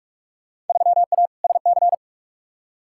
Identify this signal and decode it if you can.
Morse code, heard as a keyed tone, about 46 wpm, 700 Hz. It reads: 3A SC
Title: Field Day Exchanges Encore - 45wpm